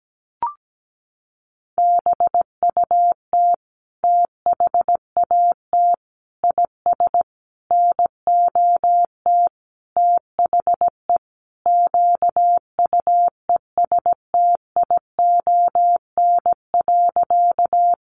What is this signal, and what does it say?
Morse code, heard as a keyed tone, about 17 wpm, 700 Hz. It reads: BUT THAT IS NOT THE QUESTION.
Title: Sentences from Top 600 Words 17wpm